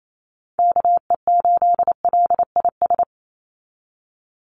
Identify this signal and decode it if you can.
Morse code, heard as a keyed tone, about 28 wpm, 700 Hz. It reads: KE8LIS
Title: USA Call Signs 28wpm